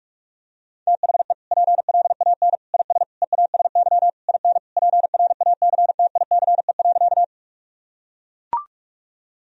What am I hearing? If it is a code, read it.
Morse code, heard as a keyed tone, about 41 wpm, 700 Hz. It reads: THE PLAN IS EASY IN PRACTICE.
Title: Sentences from Top 1000 Words - Review Part 4 - 45wpm